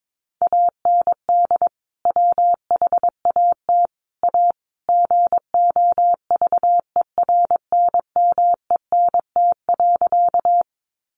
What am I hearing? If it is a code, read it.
Morse code, heard as a keyed tone, about 22 wpm, 700 Hz. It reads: AND WHAT A GOVERNMENT.
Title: Sentences from Top 600 Words 22wpm